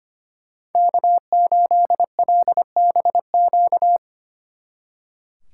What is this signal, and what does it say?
Morse code, heard as a keyed tone, about 32 wpm, 700 Hz. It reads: K8LBQ